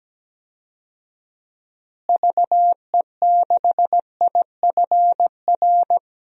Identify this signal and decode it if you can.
Morse code, heard as a keyed tone, about 17 wpm, 700 Hz. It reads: VE6IFR
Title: DXCC-100 Call Signs - 6 Characters - Repeated 3x - 17wpm